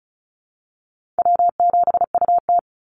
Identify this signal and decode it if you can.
Morse code, heard as a keyed tone, about 34 wpm, 700 Hz. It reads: W7UT
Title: Call Signs Repeated 3x - 35wpm